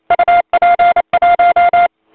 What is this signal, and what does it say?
Morse code, heard as a keyed tone, about 27 wpm, 700 Hz. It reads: UP1
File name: voice / 80m j3 pa5et 002 2001 08 08